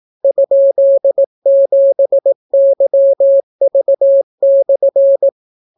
Morse code, heard as a keyed tone, about 18 wpm, 550 Hz. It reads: ?7YV/